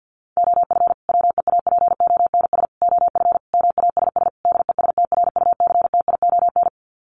Morse code, heard as a keyed tone, about 50 wpm, 700 Hz. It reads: OF WEAPONS OF MASS DESTRUCTION
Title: Sets of 5 Words Encore 50wpm